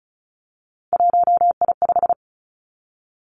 Morse code, heard as a keyed tone, about 36 wpm, 700 Hz. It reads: 1I5
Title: Three Challenging Letter-Number Combinations 35wpm